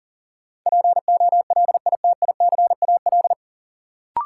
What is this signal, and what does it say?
Morse code, heard as a keyed tone, about 37 wpm, 700 Hz. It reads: POLITICAL
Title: Sentences from Top 800 Words - New Word Drill - 40wpm